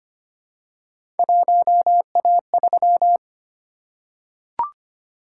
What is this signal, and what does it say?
Morse code, heard as a keyed tone, about 25 wpm, 700 Hz. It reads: 1A3
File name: Three Letter-Number Combinations - 25wpm